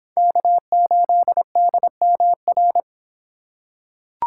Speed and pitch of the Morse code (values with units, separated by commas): 24 wpm, 700 Hz